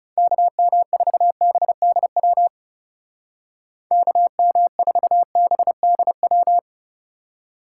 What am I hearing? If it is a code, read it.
Morse code, heard as a keyed tone, about 36 wpm, 700 Hz. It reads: KM4BDW KM4BDW
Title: Speed Racing - USA Call Signs 30wpm